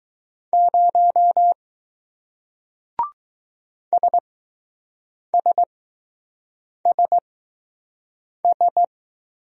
Morse code, heard as a keyed tone, about 20 wpm, spaced as for 6 wpm, 700 Hz. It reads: 0 SSSS